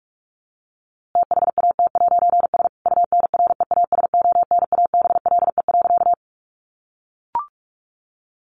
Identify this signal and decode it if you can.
Morse code, heard as a keyed tone, about 46 wpm, 700 Hz. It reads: THAT'S UNREASONABLE.
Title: A Sound Of Thunder 45wpm